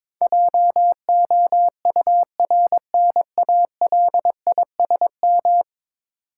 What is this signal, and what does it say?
Morse code, heard as a keyed tone, about 22 wpm, 700 Hz. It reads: JOURNALISM
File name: practice / 22wpm / 10 Letter Words - First 3 Letters Given - 22wpm